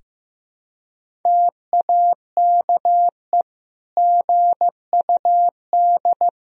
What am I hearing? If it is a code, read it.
Morse code, heard as a keyed tone, about 15 wpm, 700 Hz. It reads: TAKE GUD
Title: Sentences from Top 500 Words and Abbreviations 15wpm